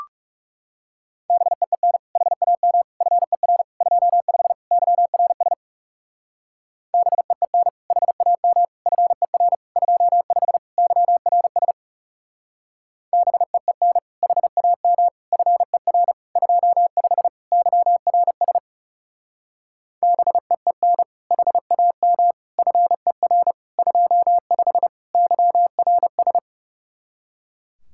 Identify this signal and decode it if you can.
Morse code, heard as a keyed tone, about 35 wpm, 700 Hz. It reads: BEEN HAM FER 25 YRS BEEN HAM FER 25 YRS BEEN HAM FER 25 YRS BEEN HAM FER 25 YRS
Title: Speed-Racing Common QSO Phrases - 30wpm